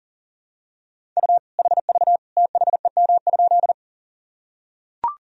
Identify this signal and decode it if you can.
Morse code, heard as a keyed tone, about 40 wpm, 700 Hz. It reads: U HV THEM?
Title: Sentences from Top 100 Words and Abbreviations 40wpm